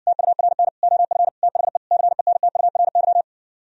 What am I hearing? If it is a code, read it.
Morse code, heard as a keyed tone, about 59 wpm, 700 Hz. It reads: TURN OF THE CENTURY